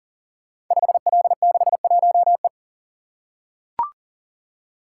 Morse code, heard as a keyed tone, about 38 wpm, 700 Hz. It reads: 5L61E
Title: Code Groups - 5 Letter:Numbers - 40wpm